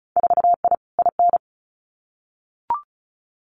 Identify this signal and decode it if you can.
Morse code, heard as a keyed tone, about 36 wpm, 700 Hz. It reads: 4I IN
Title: Winter Field Day Exchanges 35wpm